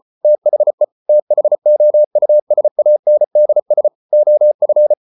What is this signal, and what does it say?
Morse code, heard as a keyed tone, about 34 wpm, 600 Hz. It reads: THE THOUSANDS OF